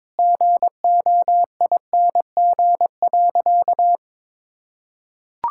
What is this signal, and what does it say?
Morse code, heard as a keyed tone, about 22 wpm, 700 Hz. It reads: GOING.
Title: Sentences from Top 1000 Words - Review Part 5 - 22wpm